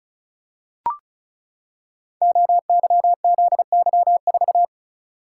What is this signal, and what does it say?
Morse code, heard as a keyed tone, about 36 wpm, 700 Hz. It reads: OYZY4
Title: Code Groups - 5 Letter:Numbers - 35wpm